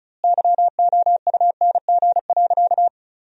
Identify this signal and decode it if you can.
Morse code, heard as a keyed tone, about 36 wpm, 700 Hz. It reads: YOUNG.
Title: 2x Word-Spacing - Sentences from Top 400 Words - 35wpm